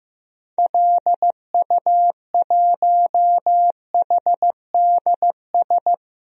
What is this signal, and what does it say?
Morse code, heard as a keyed tone, about 15 wpm, 700 Hz. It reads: LU1HDS
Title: DXCC-100 Call Signs - 6 Characters - Repeated 3x - 15wpm